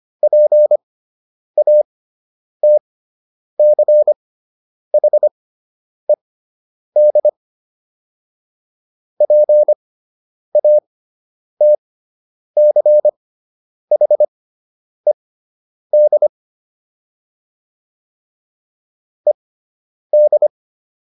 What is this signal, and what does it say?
Morse code, heard as a keyed tone, about 25 wpm, spaced as for 9 wpm, 600 Hz. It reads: PATCHED PATCHED ED